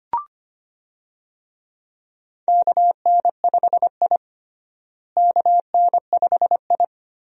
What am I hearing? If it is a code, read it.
Morse code, heard as a keyed tone, about 25 wpm, 700 Hz. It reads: KN5I KN5I
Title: USA Call Signs - 4 Characters - Repeated 2x - 25wpm